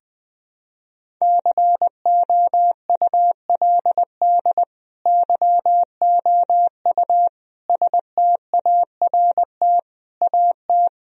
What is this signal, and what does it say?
Morse code, heard as a keyed tone, about 20 wpm, 700 Hz. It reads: COULD YOU START AT
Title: Sentences from Top 1000 Words - Review Part 1 - 20wpm